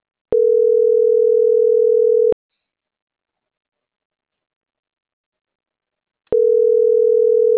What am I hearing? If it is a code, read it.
Ringback tone (North America), 440 + 480 Hz.